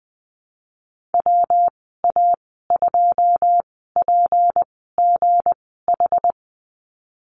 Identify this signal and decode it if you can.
Morse code, heard as a keyed tone, about 20 wpm, spaced as for 15 wpm, 700 Hz. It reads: WA2PGH
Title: 167 Lesson 27g - Number 7 - Call Signs 20wpm